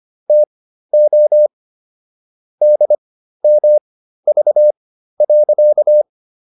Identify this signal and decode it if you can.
Morse code, heard as a keyed tone, about 25 wpm, spaced as for 13 wpm, 600 Hz. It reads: TO DMV.